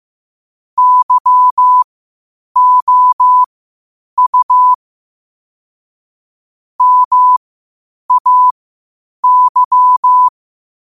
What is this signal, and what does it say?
Morse code, heard as a keyed tone, about 15 wpm, spaced as for 8 wpm, 1000 Hz. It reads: YOU MAY
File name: qotdfiles / 8wpm